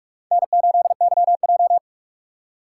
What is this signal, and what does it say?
Morse code, heard as a keyed tone, about 42 wpm, 690 Hz. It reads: N8YJ
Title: USA Call Signs - 4x Character Spacing - 45wpm